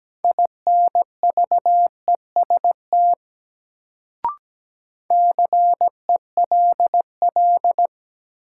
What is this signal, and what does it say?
Morse code, heard as a keyed tone, about 17 wpm, 700 Hz. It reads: INVEST CELL
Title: Sentences from Top 1100 Words - New Word Drill Alt - 17wpm